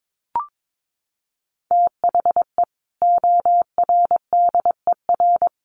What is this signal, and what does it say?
Morse code, heard as a keyed tone, about 22 wpm, 700 Hz. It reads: THE ORDER
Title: Sentences from Top 1200 Words 22wpm